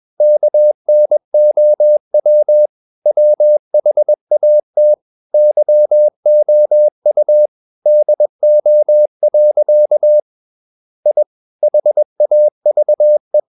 Morse code, heard as a keyed tone, about 21 wpm, 600 Hz. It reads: KNOW WHAT YOU DO. I HAVE
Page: CW med Gnister